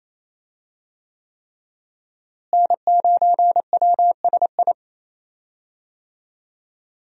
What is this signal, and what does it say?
Morse code, heard as a keyed tone, about 28 wpm, 700 Hz. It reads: N9WSI
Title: USA Call Signs - 5 Characters - Repeated 3x - 28wpm